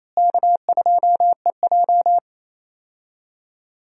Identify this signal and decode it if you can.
Morse code, heard as a keyed tone, about 28 wpm, 700 Hz. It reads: K2EJ
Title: USA Call Signs - 4 Characters - Repeated 3x - 28wpm